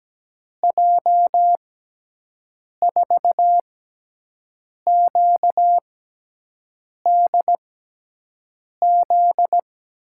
Morse code, heard as a keyed tone, about 17 wpm, spaced as for 6 wpm, 700 Hz. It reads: J4QDZ
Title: USA Call Signs - 6 Characters - 6x Character Spacing - 17wpm